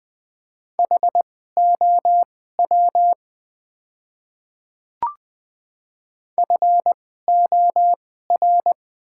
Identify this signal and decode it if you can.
Morse code, heard as a keyed tone, about 20 wpm, spaced as for 15 wpm, 700 Hz. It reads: HOW FOR
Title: 2x Character Spacing - Top 100 Words - 20wpm